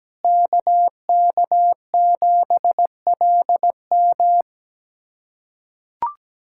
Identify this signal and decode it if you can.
Morse code, heard as a keyed tone, about 17 wpm, 700 Hz. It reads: KK7LM
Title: USA Callsign DE Callsign - 5 Characters - 17wpm